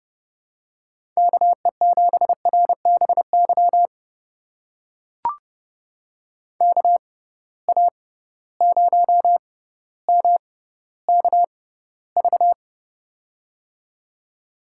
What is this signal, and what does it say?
Morse code, heard as a keyed tone, about 30 wpm, spaced as for 15 wpm, 700 Hz. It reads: KE7RBY KA0MKV
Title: USA Call Signs - 6 Characters - 6x Character Spacing - 30wpm